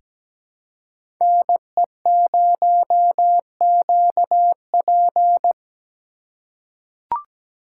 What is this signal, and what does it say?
Morse code, heard as a keyed tone, about 17 wpm, 700 Hz. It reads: NE0QP
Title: USA Call Signs - 4x Character Spacing - 17wpm